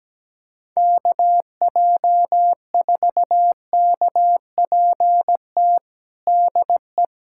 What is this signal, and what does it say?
Morse code, heard as a keyed tone, about 17 wpm, 700 Hz. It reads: KJ4KPT DE
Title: USA Callsign DE Callsign - 6 Characters - 17wpm